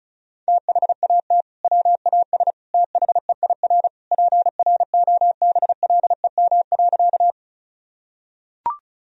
Morse code, heard as a keyed tone, about 34 wpm, 700 Hz. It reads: THAT WAS THEIR PROBLEM.
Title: Sentences from Top 1000 Words - Review Part 2 - 35wpm